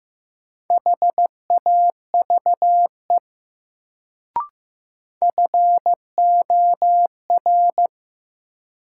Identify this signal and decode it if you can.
Morse code, heard as a keyed tone, about 15 wpm, 700 Hz. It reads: HAVE FOR
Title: Sets of 2 Words - New Word Drill - 15wpm